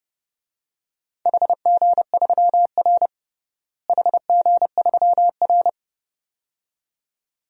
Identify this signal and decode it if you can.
Morse code, heard as a keyed tone, about 32 wpm, 700 Hz. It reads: HG3R HG3R
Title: DXCC-100 Call Signs - 4 Characters - Repeated 2x - 30wpm